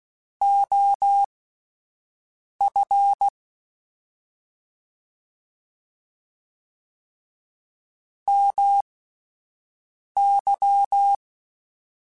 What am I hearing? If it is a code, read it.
Morse code, heard as a keyed tone, about 16 wpm, spaced as for 5 wpm, 800 Hz. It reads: OF MY
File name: (spaced) / QSOs Spaced-mp3s / WD0FBT 1 @10 sp